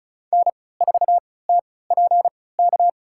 Morse code, heard as a keyed tone, about 35 wpm, spaced as for 20 wpm, 700 Hz. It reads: N4TPK
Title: USA Call Signs - 5 Characters - 3x Character Spacing - 35wpm